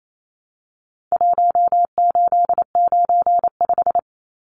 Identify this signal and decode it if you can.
Morse code, heard as a keyed tone, about 28 wpm, 700 Hz. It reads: 1895
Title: Numbers - Contest Style - 4-Digit 28wpm